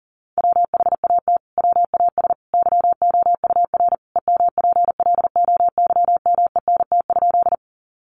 Morse code, heard as a keyed tone, about 40 wpm, 700 Hz. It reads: WHAT WAS YOUR EMPLOYMENT?
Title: Sentences from Top 1500 Words 40wpm